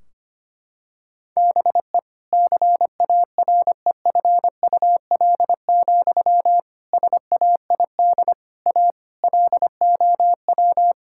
Morse code, heard as a keyed tone, about 25 wpm, 700 Hz. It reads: BE CAREFUL, SAID A LOW
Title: Sentences from Top 1600 Words 25wpm